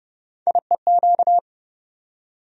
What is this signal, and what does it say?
Morse code, heard as a keyed tone, about 30 wpm, 700 Hz. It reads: IEQ
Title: USA Call Sign Postfixes - 3 Letters 30wpm